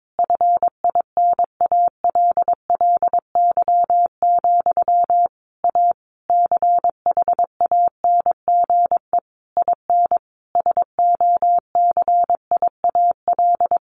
Morse code, heard as a keyed tone, about 22 wpm, 700 Hz. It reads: FINALLY, A CHANGE IN SOCIAL
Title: A Sound Of Thunder 22wpm